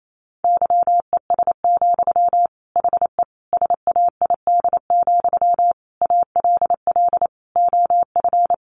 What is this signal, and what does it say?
Morse code, heard as a keyed tone, about 28 wpm, 700 Hz. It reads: YES, HE SAID, ALL OF
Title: Sentences from Top 500 Words 28wpm